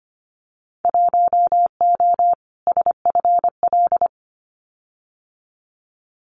Morse code, heard as a keyed tone, about 25 wpm, 700 Hz. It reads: 1O SFL
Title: Winter Field Day Exchanges 25wpm